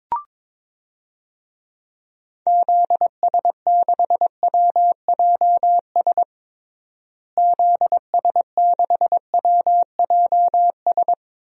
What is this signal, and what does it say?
Morse code, heard as a keyed tone, about 22 wpm, 700 Hz. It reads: ZS6WJS ZS6WJS
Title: DXCC-100 Call Signs - 6 Characters - Repeated 3x - 22wpm